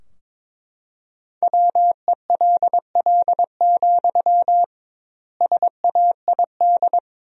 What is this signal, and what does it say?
Morse code, heard as a keyed tone, about 22 wpm, 700 Hz. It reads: WELL, SAID
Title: 2x Word-Spacing - Sentences from Top 400 Words - 22wpm